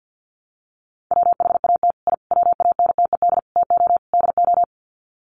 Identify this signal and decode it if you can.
Morse code, heard as a keyed tone, about 50 wpm, 700 Hz. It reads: WHAT I WANTED TO DO
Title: Sets of 5 Words Encore2 50wpm